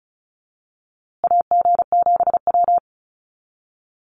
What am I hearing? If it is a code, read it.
Morse code, heard as a keyed tone, about 34 wpm, 700 Hz. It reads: AG7W
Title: USA Call Signs Repeated 3x 35wpm